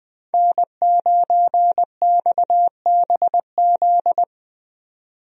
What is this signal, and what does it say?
Morse code, heard as a keyed tone, about 20 wpm, 700 Hz. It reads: N9XBZ
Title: USA Call Signs - 5 Characters - 20wpm